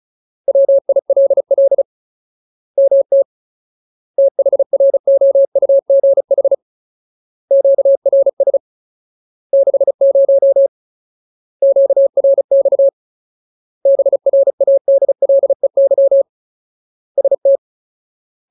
Morse code, heard as a keyed tone, about 34 wpm, 550 Hz. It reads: WILL MT THROUGH QRS 60 QRX BRADLEY ST
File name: Ultimate Word List / 35WPM Ultimate